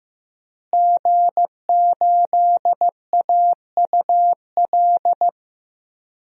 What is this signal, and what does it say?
Morse code, heard as a keyed tone, about 15 wpm, 700 Hz. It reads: G8AUL
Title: DXCC-100 Call Signs - 5 Characters - Repeated 2x - 15wpm